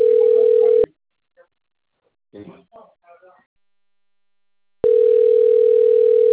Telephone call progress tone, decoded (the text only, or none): ringback tone